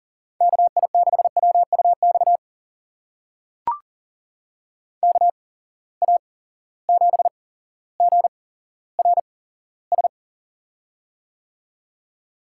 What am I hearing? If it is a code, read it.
Morse code, heard as a keyed tone, about 40 wpm, spaced as for 17 wpm, 700 Hz. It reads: KI6WUX KA7GRS
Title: USA Call Signs - 6 Characters - 8x Character Spacing - 40wpm